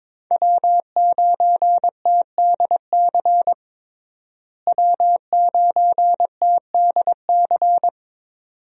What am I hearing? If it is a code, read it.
Morse code, heard as a keyed tone, about 22 wpm, 700 Hz. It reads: W9TDC W9TDC
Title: USA Call Signs - 5 Characters - Repeated 2x - 22wpm